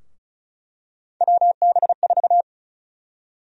Morse code, heard as a keyed tone, about 34 wpm, 700 Hz. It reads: WB4
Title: USA Call Sign Prefixes - Repeated 2x - 35wpm